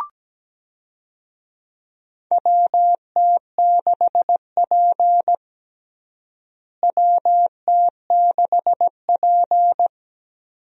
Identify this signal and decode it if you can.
Morse code, heard as a keyed tone, about 17 wpm, 700 Hz. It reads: WT6P WT6P
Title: USA Call Signs - 4 Characters - Repeated 2x - 17wpm